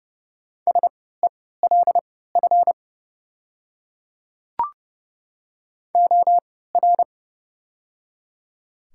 Morse code, heard as a keyed tone, about 32 wpm, spaced as for 17 wpm, 700 Hz. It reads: SELF OR
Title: 3x Character Spacing - Top 300 Words - 30wpm